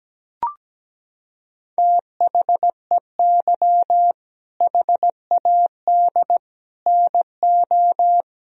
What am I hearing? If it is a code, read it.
Morse code, heard as a keyed tone, about 17 wpm, 700 Hz. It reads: THEY HAD NO
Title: Sentences from Top 1000 Words - Review Part 1 - 17wpm